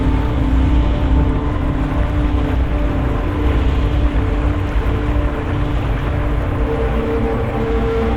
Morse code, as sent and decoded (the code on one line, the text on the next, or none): none